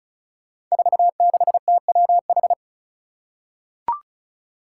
Morse code, heard as a keyed tone, about 36 wpm, 700 Hz. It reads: 46TWH
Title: Five Challenging Letter-Number Combinations 35wpm